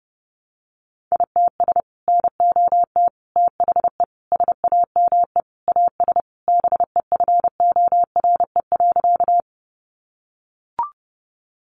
Morse code, heard as a keyed tone, about 30 wpm, 700 Hz. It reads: ITS NOT THE SAME AS BEFORE.
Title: Sentences from Top 600 Words 30wpm